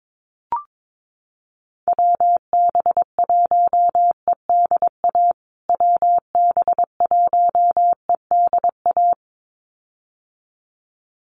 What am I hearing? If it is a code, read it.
Morse code, heard as a keyed tone, about 22 wpm, 700 Hz. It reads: WB1EDA WB1EDA
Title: USA Call Signs Repeated 2x 22wpm